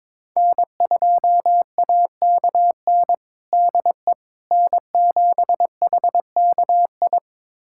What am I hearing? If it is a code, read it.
Morse code, heard as a keyed tone, about 22 wpm, 700 Hz. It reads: N2AKN DE N7HKI
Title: USA Callsign DE Callsign - 5 Characters - 22wpm